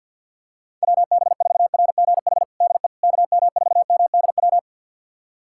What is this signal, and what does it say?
Morse code, heard as a keyed tone, about 49 wpm, 700 Hz. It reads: WB4RGH DE KM4MDW
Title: USA Callsign DE Callsign 50wpm